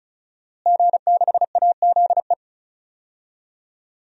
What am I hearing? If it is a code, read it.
Morse code, heard as a keyed tone, about 36 wpm, 700 Hz. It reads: G6AZE
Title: DXCC-100 Call Signs - 5 Characters - Repeated 2x - 35wpm